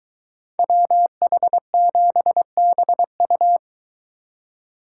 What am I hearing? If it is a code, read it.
Morse code, heard as a keyed tone, about 23 wpm, 700 Hz. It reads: WH7BU